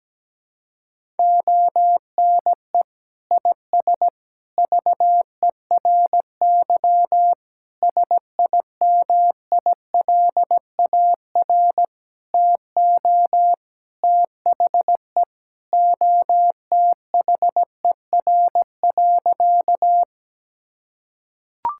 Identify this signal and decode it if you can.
Morse code, heard as a keyed tone, about 17 wpm, 700 Hz. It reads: ONE IS VERY SIMILAR TO THE OTHER.
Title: Sentences from Top 1000 Words - Review Part 5 - 17wpm